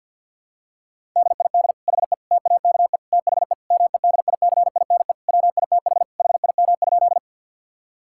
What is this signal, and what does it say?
Morse code, heard as a keyed tone, about 50 wpm, 700 Hz. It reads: DID HE TAKE THE MEDICINE WITH HIM?
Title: Sentences from Top 1000 Words - Review Part 6 - 50wpm